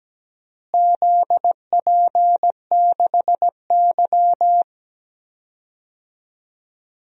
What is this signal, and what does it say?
Morse code, heard as a keyed tone, about 17 wpm, 700 Hz. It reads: ZP6Y
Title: DXCC-100 Call Signs - 4 Characters - Repeated 2x - 17wpm